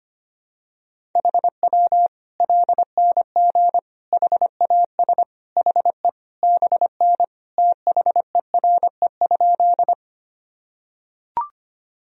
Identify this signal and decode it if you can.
Morse code, heard as a keyed tone, about 25 wpm, 700 Hz. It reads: HW LNG HAS HE BN THERE?
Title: Sentences from Top 100 Words and Abbreviations 25wpm